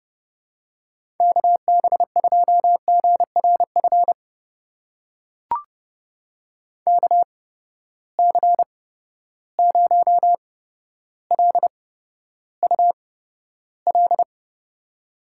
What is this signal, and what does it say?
Morse code, heard as a keyed tone, about 30 wpm, spaced as for 13 wpm, 700 Hz. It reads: KB2GRF KC0LUL
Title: USA Call Signs - 6 Characters - 8x Character Spacing - 30wpm